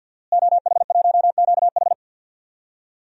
Morse code, heard as a keyed tone, about 48 wpm, 700 Hz. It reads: OH1QH
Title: DXCC-100 Call Signs - 5 Characters - Repeated 3x - 50wpm